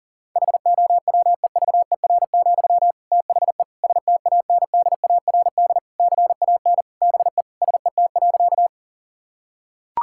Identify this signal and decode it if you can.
Morse code, heard as a keyed tone, about 37 wpm, 700 Hz. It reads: HOWEVER, THE STANDARD CAN BE SET.
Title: Sentences from Top 1000 Words - Review Part 6 - 40wpm